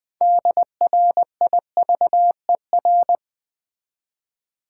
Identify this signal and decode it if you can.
Morse code, heard as a keyed tone, about 20 wpm, 700 Hz. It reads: DRIVER
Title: Build A Word - 2 Letters At A Time - 6-Letter Words - 20wpm